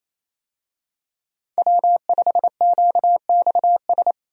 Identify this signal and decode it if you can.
Morse code, heard as a keyed tone, about 28 wpm, 700 Hz. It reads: W5QXS
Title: USA Call Signs - 5 Characters - Repeated 3x - 28wpm